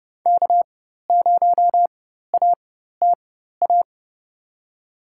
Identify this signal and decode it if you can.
Morse code, heard as a keyed tone, about 30 wpm, spaced as for 14 wpm, 700 Hz. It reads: K0ATA
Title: USA Call Signs - 5 Characters - 4x Character Spacing - 30wpm